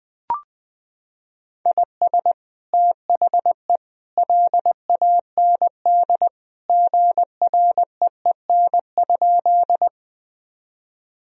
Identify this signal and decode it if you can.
Morse code, heard as a keyed tone, about 21 wpm, 700 Hz. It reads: IS THE LAND GREEN?
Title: Sentences from Top 1000 Words - Review Part 2 - 20wpm